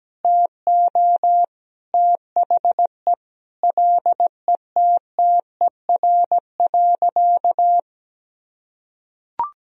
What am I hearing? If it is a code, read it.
Morse code, heard as a keyed tone, about 17 wpm, 700 Hz. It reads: TO THE LETTER.
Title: Sentences from Top 1000 Words - Review Part 1 - 17wpm